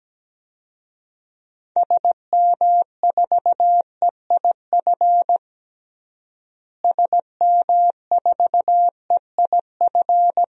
Morse code, heard as a keyed tone, about 17 wpm, 700 Hz. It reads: SM4EIF SM4EIF
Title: DXCC-100 Call Signs - 6 Characters - Repeated 3x - 17wpm